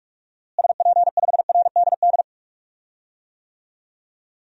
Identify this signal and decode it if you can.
Morse code, heard as a keyed tone, about 42 wpm, 700 Hz. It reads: SP5RDD
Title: DXCC-100 Call Signs - 6 Characters - Repeated 2x - 45wpm